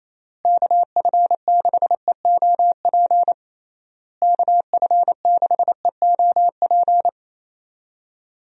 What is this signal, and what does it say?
Morse code, heard as a keyed tone, about 28 wpm, 700 Hz. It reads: KF6EOP KF6EOP
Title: USA Call Signs - 6 Characters - Repeated 3x - 28wpm